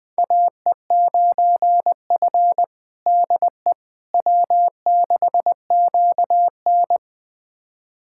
Morse code, heard as a keyed tone, about 20 wpm, 700 Hz. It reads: AE9F DE W6QN
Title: USA Callsign DE Callsign - 4 Characters - 20wpm